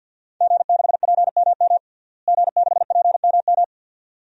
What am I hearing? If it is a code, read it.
Morse code, heard as a keyed tone, about 49 wpm, 700 Hz. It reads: G6PMM G6PMM
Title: DXCC-100 Call Signs - 5 Characters - Repeated 2x - 50wpm